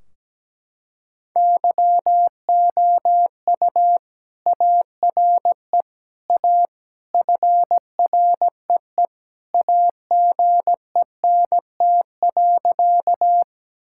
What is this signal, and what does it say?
Morse code, heard as a keyed tone, about 17 wpm, 700 Hz. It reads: YOU ARE A FREE AGENT.
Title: Sentences from Top 1600 Words 17wpm